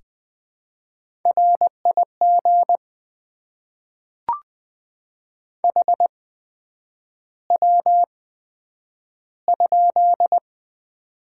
Morse code, heard as a keyed tone, about 20 wpm, spaced as for 9 wpm, 700 Hz. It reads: RIG HW?
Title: Top 20 QSO Elements - 8x Character Spacing - 20wpm